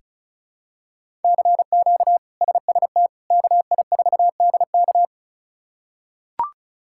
Morse code, heard as a keyed tone, about 34 wpm, 700 Hz. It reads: CQ SST KI4DK
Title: K1USN SST QSO Practice - 35wpm